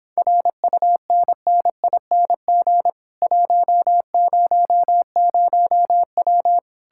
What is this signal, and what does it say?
Morse code, heard as a keyed tone, about 29 wpm, 700 Hz. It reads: RUNNING 100W